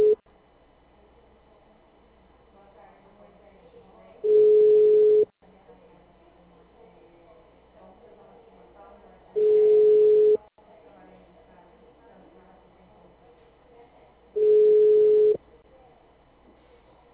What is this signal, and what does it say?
Ringback tone (Europe, 425 Hz), with speech.